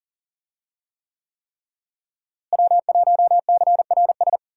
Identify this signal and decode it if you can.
Morse code, heard as a keyed tone, about 40 wpm, 700 Hz. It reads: W1CRS
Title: USA Call Signs - 5 Characters - Repeated 3x - 40wpm